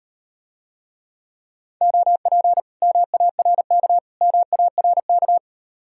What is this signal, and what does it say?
Morse code, heard as a keyed tone, about 38 wpm, 700 Hz. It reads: OP MARK MARK